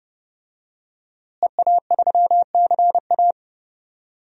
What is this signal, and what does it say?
Morse code, heard as a keyed tone, about 31 wpm, 700 Hz. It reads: EA3CA
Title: DXCC-100 Call Signs - 5 Characters - Repeated 3x - 30wpm